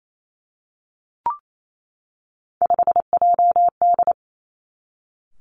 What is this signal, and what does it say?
Morse code, heard as a keyed tone, about 28 wpm, 700 Hz. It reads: 5JD
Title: Three Challenging Letter-Number Combinations 28wpm